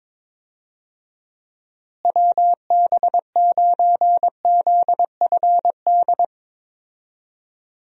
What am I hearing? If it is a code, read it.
Morse code, heard as a keyed tone, about 22 wpm, 700 Hz. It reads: WB9ZFD
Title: USA Call Signs - 6 Characters - Repeated 2x - 22wpm